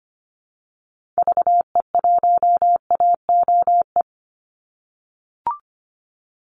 Morse code, heard as a keyed tone, about 25 wpm, 700 Hz. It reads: VE1AOE
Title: Call Signs - 25wpm